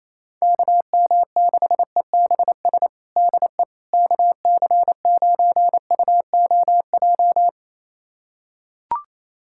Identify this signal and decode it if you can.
Morse code, heard as a keyed tone, about 27 wpm, 700 Hz. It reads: KM6EBS DE KC9UOJ
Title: USA Callsign DE Callsign - 6 Characters - 28wpm